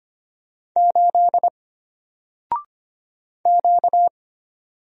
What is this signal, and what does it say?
Morse code, heard as a keyed tone, about 25 wpm, 700 Hz. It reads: 8 Q
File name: Single-Letter-Number-25wpm